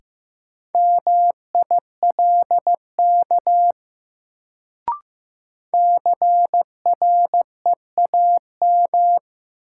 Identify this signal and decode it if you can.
Morse code, heard as a keyed tone, about 15 wpm, 700 Hz. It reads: MILK CREAM
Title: Sentences from Top 1500 Words - New Word Drill Alt - 15wpm